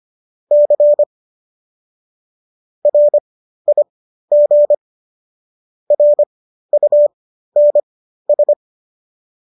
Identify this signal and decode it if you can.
Morse code, heard as a keyed tone, about 25 wpm, spaced as for 13 wpm, 590 Hz. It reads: C RIG RUNS